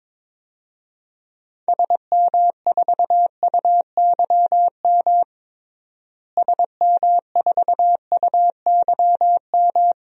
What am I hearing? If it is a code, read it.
Morse code, heard as a keyed tone, about 22 wpm, 700 Hz. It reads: SM4UYM SM4UYM
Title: DXCC-100 Call Signs - 6 Characters - Repeated 2x - 22wpm